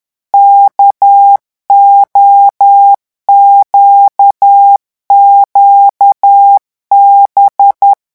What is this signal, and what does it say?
Morse code, heard as a keyed tone, about 10 wpm, 800 Hz. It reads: KOQQB
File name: morse mp3 / revision13-vitesse 10 mots